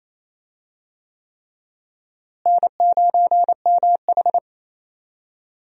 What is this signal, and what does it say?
Morse code, heard as a keyed tone, about 28 wpm, 700 Hz. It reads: N9MH
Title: USA Call Signs - 4 Characters - Repeated 3x - 28wpm